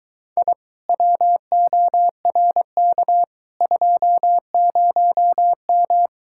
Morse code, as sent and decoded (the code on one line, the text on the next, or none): .. / .-- --- .-. -.- / ..--- ----- --
I WORK 20M